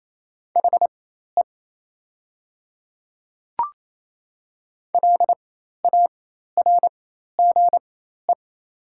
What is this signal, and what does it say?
Morse code, heard as a keyed tone, about 27 wpm, spaced as for 13 wpm, 700 Hz. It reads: HE LARGE
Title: 4x Character Spacing - Top 300 Words - 28wpm